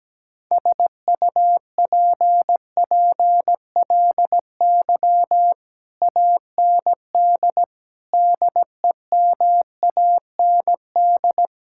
Morse code, heard as a keyed tone, about 17 wpm, 700 Hz. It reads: SUPPLY AND DEMAND
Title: Sentences from Top 1000 Words - Review Part 6 - 17wpm